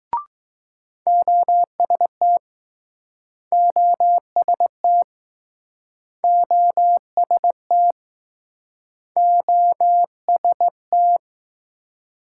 Morse code, heard as a keyed tone, about 20 wpm, 700 Hz. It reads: OST OST OST OST